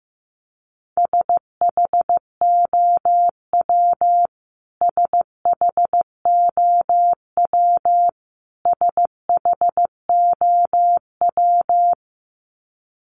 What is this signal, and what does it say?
Morse code, heard as a keyed tone, about 15 wpm, 700 Hz. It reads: SHOW SHOW SHOW
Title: Top 500 Words Repeated 15wpm